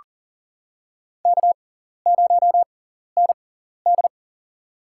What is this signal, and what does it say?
Morse code, heard as a keyed tone, about 43 wpm, spaced as for 14 wpm, 690 Hz. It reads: K0ND
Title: USA Call Signs - 4 Characters - 6x Character Spacing - 40wpm